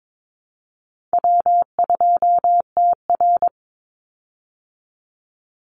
Morse code, heard as a keyed tone, about 22 wpm, 700 Hz. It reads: W2TR
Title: Easy Call Signs Repeated 3x - 22wpm